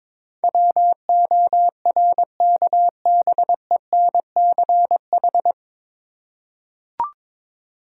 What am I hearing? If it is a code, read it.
Morse code, heard as a keyed tone, about 21 wpm, 690 Hz. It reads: WORKBENCH
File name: Difficult Words with Hints - 22wpm